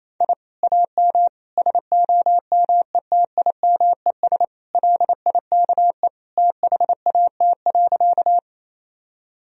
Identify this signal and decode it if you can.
Morse code, heard as a keyed tone, about 28 wpm, 700 Hz. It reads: I AM SOMETIMES LIKE THAT.
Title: Sentences from Top 1000 Words - Review Part 3 - 28wpm